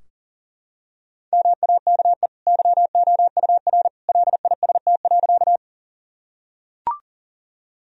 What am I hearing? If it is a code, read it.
Morse code, heard as a keyed tone, about 40 wpm, 700 Hz. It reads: MAKE YOUR LIST.
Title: Sentences from Top 1000 Words - Review Part 2 - 40wpm